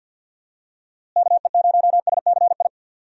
Morse code, heard as a keyed tone, about 51 wpm, 700 Hz. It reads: KE0SCI
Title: USA Call Signs - 6 Characters - Repeated 2x - 50wpm